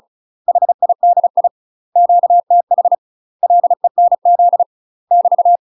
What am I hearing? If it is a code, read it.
Morse code, heard as a keyed tone, about 36 wpm, 700 Hz. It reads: HIDI QTH LENZ <BT>
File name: UK MASTER QSOs TRAINING / 35WPM / UK MASTER QSOs 0010